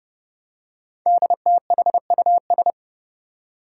Morse code, heard as a keyed tone, about 28 wpm, 700 Hz. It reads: DTHUS
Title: Five Challenging Letter-Number Combinations 30wpm